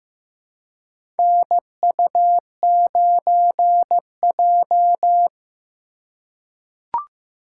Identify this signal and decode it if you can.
Morse code, heard as a keyed tone, about 15 wpm, 700 Hz. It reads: NU9J
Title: USA Call Signs - 4 Characters - 4x Character Spacing - 15wpm